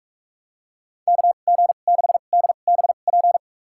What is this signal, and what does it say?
Morse code, heard as a keyed tone, about 41 wpm, spaced as for 31 wpm, 700 Hz. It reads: KG6DBP
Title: USA Call Signs - 6 Characters - 2x Character Spacing - 45wpm